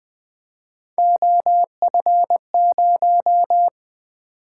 Morse code, heard as a keyed tone, about 20 wpm, 700 Hz. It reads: OF0
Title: Three Letter-Number Combinations - 20wpm